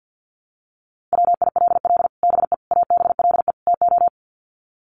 Morse code, heard as a keyed tone, about 50 wpm, 700 Hz. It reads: WILL BE ABLE TO
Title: Sets of 4 Words 50wpm